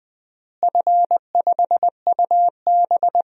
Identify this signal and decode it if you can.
Morse code, heard as a keyed tone, about 20 wpm, 700 Hz. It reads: F5UB
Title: DXCC-100 Call Signs - 4 Characters - Repeated 3x - 20wpm